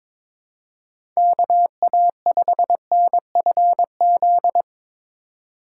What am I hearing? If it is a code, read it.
Morse code, heard as a keyed tone, about 24 wpm, 700 Hz. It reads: KA5NFZ
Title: Speed Racing - USA 2x3 Call Signs 22wpm